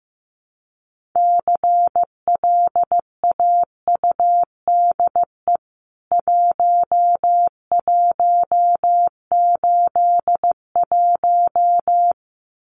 Morse code, heard as a keyed tone, about 15 wpm, 700 Hz. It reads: CLAUDE 1181
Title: CWT Exchange 15wpm